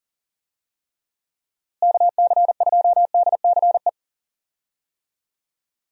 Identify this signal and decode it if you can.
Morse code, heard as a keyed tone, about 38 wpm, 700 Hz. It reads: KC2DCE
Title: USA Call Signs - 6 Characters - Repeated 3x - 40wpm